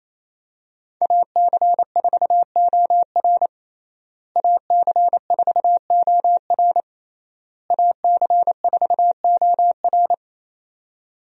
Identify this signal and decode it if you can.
Morse code, heard as a keyed tone, about 28 wpm, 700 Hz. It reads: AC4OR AC4OR AC4OR
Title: USA Call Signs - 5 Characters - Repeated 3x - 28wpm